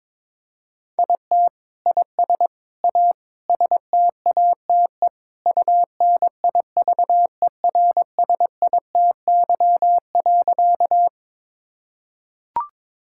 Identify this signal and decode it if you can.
Morse code, heard as a keyed tone, about 21 wpm, 700 Hz. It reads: IT IS A STATE UNIVERSITY.
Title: Sentences from Top 1000 Words - Review Part 5 - 22wpm